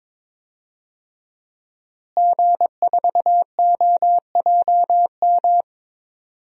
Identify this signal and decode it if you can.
Morse code, heard as a keyed tone, about 22 wpm, 700 Hz. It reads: G4OJM
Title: DXCC-100 Call Signs - 5 Characters - Repeated 3x - 22wpm